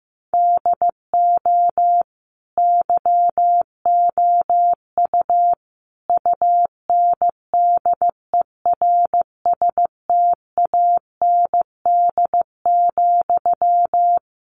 Morse code, heard as a keyed tone, about 15 wpm, 700 Hz. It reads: DO YOU UNDERSTAND,
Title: Sentences from Top 1200 Words 15wpm